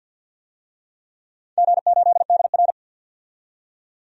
Morse code, heard as a keyed tone, about 47 wpm, 700 Hz. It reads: G8DR
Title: DXCC-100 Call Signs - 4 Characters - 50wpm